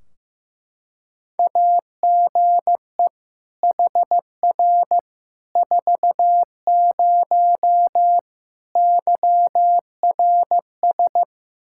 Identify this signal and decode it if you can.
Morse code, heard as a keyed tone, about 15 wpm, 700 Hz. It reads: AGE HR 40 YRS
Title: Practice QSO - Exchange 3 - More Info - 15wpm